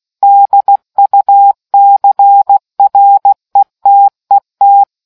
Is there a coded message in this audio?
Yes: Morse code from 0.2 to 4.8 s, heard as a keyed tone, about 16 wpm, 800 Hz. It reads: DUCRETET